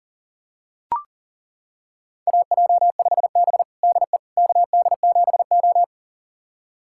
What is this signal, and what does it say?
Morse code, heard as a keyed tone, about 40 wpm, 700 Hz. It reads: AJ5B DE KD7O
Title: USA Callsign DE Callsign - 4 Characters - 40wpm